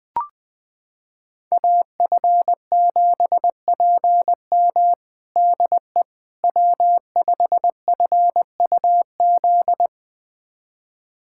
Morse code, heard as a keyed tone, about 21 wpm, 700 Hz. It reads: AF7PM DE W5FUZ
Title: USA Callsign DE Callsign - 5 Characters - 20wpm